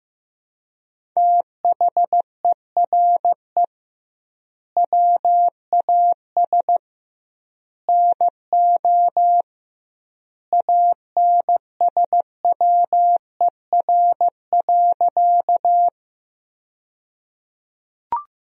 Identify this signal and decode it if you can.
Morse code, heard as a keyed tone, about 15 wpm, 700 Hz. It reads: THERE WAS NO ANSWER.
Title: 2x Word-Spacing - Sentences from Top 300 Words - 15wpm